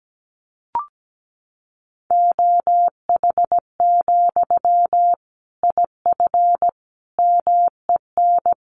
Morse code, heard as a keyed tone, about 17 wpm, 700 Hz. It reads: OH, IF MEN
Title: Sentences from Top 500 Words 17wpm